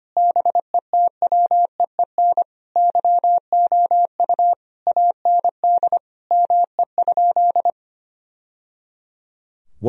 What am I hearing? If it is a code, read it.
Morse code, heard as a keyed tone, about 25 wpm, 700 Hz. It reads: BETWEEN YOU AND ME?
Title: Sentences from Top 1000 Words - Review Part 3 - 25wpm